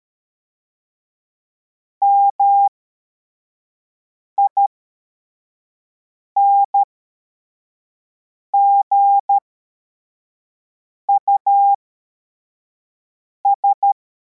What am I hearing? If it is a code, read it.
Morse code, heard as a keyed tone, about 13 wpm, spaced as for 4 wpm, 800 Hz. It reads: MINGUS